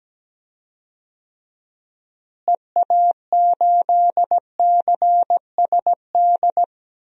Morse code, heard as a keyed tone, about 17 wpm, 700 Hz. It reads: EA8CSD
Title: DXCC-100 Call Signs - 6 Characters - Repeated 3x - 17wpm